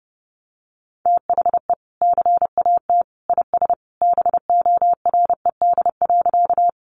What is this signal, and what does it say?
Morse code, heard as a keyed tone, about 30 wpm, 700 Hz. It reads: THE CAT IS BORED.
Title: Easy Phrases 30wpm